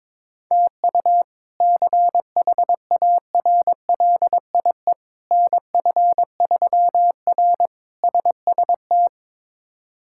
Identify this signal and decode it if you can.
Morse code, heard as a keyed tone, about 22 wpm, 700 Hz. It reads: TU CHARLIE NF3R SST
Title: K1USN SST QSO Practice - 22wpm